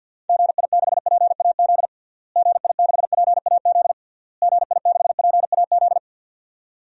Morse code, heard as a keyed tone, about 47 wpm, 700 Hz. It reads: GI6PAZ GI6PAZ GI6PAZ
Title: DXCC-100 Call Signs - 6 Characters - Repeated 3x - 50wpm